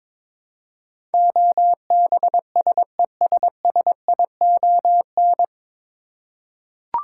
Morse code, heard as a keyed tone, about 22 wpm, 700 Hz. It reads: OBSESSION
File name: 9 Letter Words - First 3 Letters Given - 22wpm